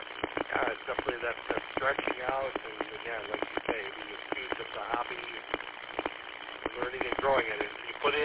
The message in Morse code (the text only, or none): none